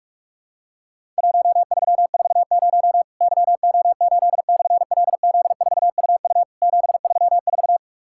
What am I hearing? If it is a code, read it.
Morse code, heard as a keyed tone, about 44 wpm, 700 Hz. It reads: 1340 YO8/LZ4UU 734
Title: Worked All Europe DX Contest - QTC Exchange - 45wpm